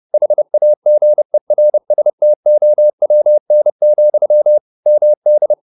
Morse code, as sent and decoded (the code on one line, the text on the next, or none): .... .- --. .-. ... - --- .-- -. --..-- / -- -..
HAGERSTOWN, MD